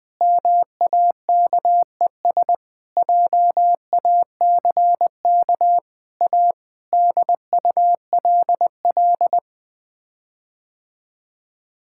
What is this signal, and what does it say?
Morse code, heard as a keyed tone, about 20 wpm, 700 Hz. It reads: MAKES JACK A DULL
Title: Sets of 8 Words - 20wpm